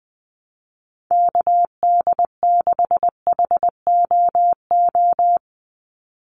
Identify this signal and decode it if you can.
Morse code, heard as a keyed tone, about 20 wpm, 700 Hz. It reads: KD6HOO
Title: USA Call Signs 20wpm